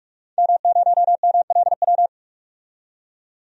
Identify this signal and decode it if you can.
Morse code, heard as a keyed tone, about 42 wpm, 700 Hz. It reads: M0MLW
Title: DXCC-100 Call Signs - 5 Characters - 45wpm